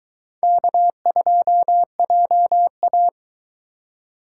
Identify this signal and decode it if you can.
Morse code, heard as a keyed tone, about 22 wpm, 700 Hz. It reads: K2JA